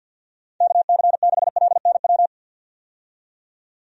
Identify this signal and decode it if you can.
Morse code, heard as a keyed tone, about 51 wpm, 700 Hz. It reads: KC6LNW
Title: USA Call Signs - 6 Characters - 50wpm